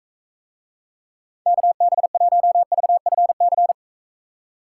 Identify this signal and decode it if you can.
Morse code, heard as a keyed tone, about 44 wpm, 700 Hz. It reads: KB1VFC